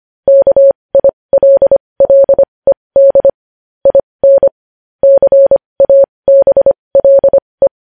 Morse code, heard as a keyed tone, about 25 wpm, spaced as for 20 wpm, 550 Hz. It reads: KILLED IN CABLE
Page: news-headlines-in-morse-code-at-20-wpm